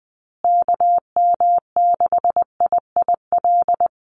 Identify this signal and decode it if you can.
Morse code, heard as a keyed tone, about 20 wpm, 700 Hz. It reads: KM6IIL
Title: USA Call Signs Repeated 2x 20wpm